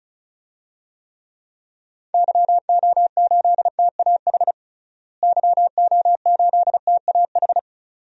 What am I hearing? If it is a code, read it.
Morse code, heard as a keyed tone, about 36 wpm, 700 Hz. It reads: YO8TAH YO8TAH
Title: DXCC-100 Call Signs - 6 Characters - Repeated 2x - 35wpm